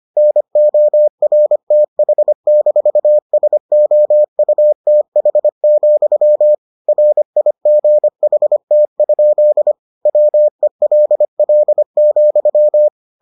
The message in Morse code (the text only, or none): NORTH-SOUTH, RIGHT? WELL,